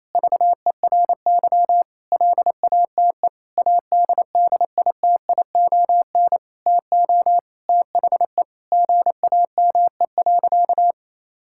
Morse code, heard as a keyed tone, about 28 wpm, 700 Hz. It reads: VERY LATE ADDITION TO THE GAME.
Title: Sentences from Top 1000 Words - Review Part 6 - 28wpm